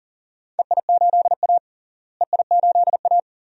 Morse code, heard as a keyed tone, about 40 wpm, 690 Hz. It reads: EI8A EI8A
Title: DXCC-100 Call Signs - 4 Characters - Repeated 2x - 40wpm